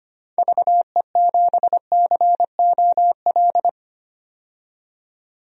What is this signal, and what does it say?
Morse code, heard as a keyed tone, about 25 wpm, 700 Hz. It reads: VE7COL
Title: DXCC-100 Call Signs - 6 Characters - 25wpm